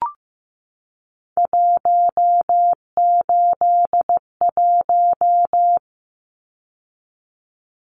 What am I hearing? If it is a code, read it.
Morse code, heard as a keyed tone, about 15 wpm, 700 Hz. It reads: 181
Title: Numbers - 3-Digit 15wpm